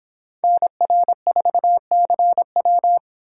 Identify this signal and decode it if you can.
Morse code, heard as a keyed tone, about 28 wpm, 700 Hz. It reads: NR4CW